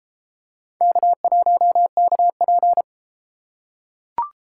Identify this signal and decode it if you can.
Morse code, heard as a keyed tone, about 31 wpm, 700 Hz. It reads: K1KP